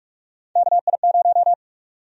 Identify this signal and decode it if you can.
Morse code, heard as a keyed tone, about 45 wpm, 700 Hz. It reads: KI0
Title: USA Call Sign Prefixes - 45wpm